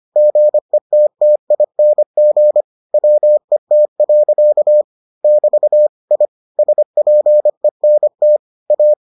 Morse code, heard as a keyed tone, about 25 wpm, 600 Hz. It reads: GETTING WET. <BT> I SPENT A